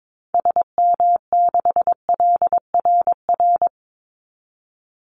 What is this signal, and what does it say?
Morse code, heard as a keyed tone, about 22 wpm, 700 Hz. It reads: SM6LRR
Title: Call Signs Repeated 3x - 22wpm